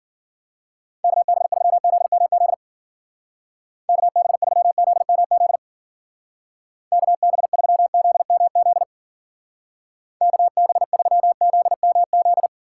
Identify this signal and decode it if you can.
Morse code, heard as a keyed tone, about 48 wpm, 700 Hz. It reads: KB3ZMZ KB3ZMZ KB3ZMZ KB3ZMZ
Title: Speed Racing - USA 2x3 Call Signs 40wpm